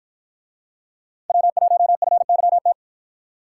Morse code, heard as a keyed tone, about 53 wpm, 700 Hz. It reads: W1FYT